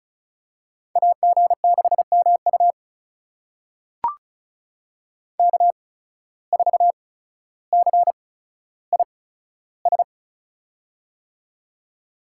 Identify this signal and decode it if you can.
Morse code, heard as a keyed tone, about 36 wpm, spaced as for 15 wpm, 700 Hz. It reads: AG6MU K4CIS
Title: USA Call Signs - 5 Characters - 8x Character Spacing - 35wpm